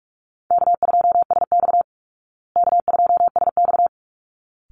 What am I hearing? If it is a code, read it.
Morse code, heard as a keyed tone, about 44 wpm, 700 Hz. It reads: K2SX K2SX
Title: Call Signs Repeated 3x - 45wpm